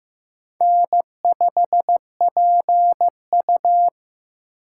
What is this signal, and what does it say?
Morse code, heard as a keyed tone, about 15 wpm, 700 Hz. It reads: N5PU
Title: K1USN SST QSO Practice - 15wpm